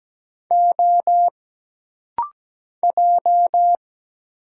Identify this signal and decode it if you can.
Morse code, heard as a keyed tone, about 17 wpm, 700 Hz. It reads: O J
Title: Single Letter-Number - Rapid-Fire 17wpm